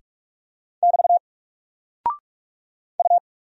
Morse code, heard as a keyed tone, about 44 wpm, 700 Hz. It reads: <BT> U
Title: Single Character (Letter:Number:Punctuation:Prosign) Alt 45wpm